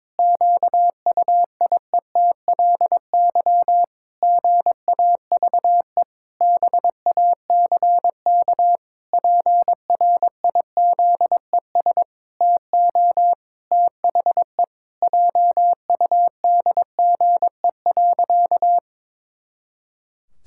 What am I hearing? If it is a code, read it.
Morse code, heard as a keyed tone, about 22 wpm, 700 Hz. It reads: QUIETLY GAVE BACK PRIZES TO THE JUDGE.
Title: Pangram Challenge v2 - 22wpm